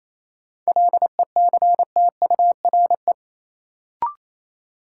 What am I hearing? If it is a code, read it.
Morse code, heard as a keyed tone, about 27 wpm, 700 Hz. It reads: LECTURE
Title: Sentences from Top 1600 Words - New Word Drill Alt - 28wpm